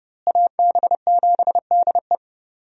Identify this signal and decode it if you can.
Morse code, heard as a keyed tone, about 32 wpm, 700 Hz. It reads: AB7DE
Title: USA Call Signs - 5 Characters - Repeated 3x - 30wpm